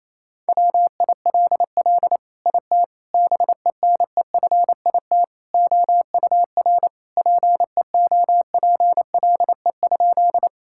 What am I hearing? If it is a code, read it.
Morse code, heard as a keyed tone, about 28 wpm, 700 Hz. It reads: WILL IT BENEFIT OUR PEOPLE?
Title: Sentences from Top 1000 Words - Review Part 6 - 28wpm